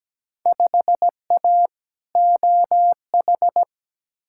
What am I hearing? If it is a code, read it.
Morse code, heard as a keyed tone, about 17 wpm, 700 Hz. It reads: 5A OH
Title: Field Day Exchanges Encore - 17wpm